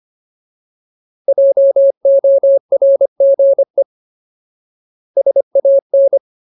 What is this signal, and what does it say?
Morse code, heard as a keyed tone, about 25 wpm, 550 Hz. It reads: JORGE SAN
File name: Ultimate Word List / 25WPM Ultimate